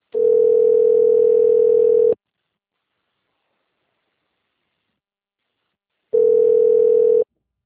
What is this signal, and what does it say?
Ringback tone (North America), 440 + 480 Hz.
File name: rg-1600-unknown- - - .58880